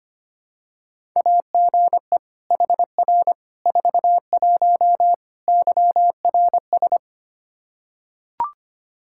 Morse code, heard as a keyed tone, about 25 wpm, 700 Hz. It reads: AGE HR 41 YRS
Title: Practice QSO - Exchange 3 - More Info - 25wpm